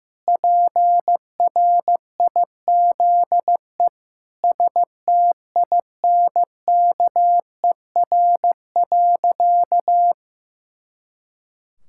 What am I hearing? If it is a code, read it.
Morse code, heard as a keyed tone, about 15 wpm, 700 Hz. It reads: PRIZE STINKER.
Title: Pangram Challenge v2 - 15wpm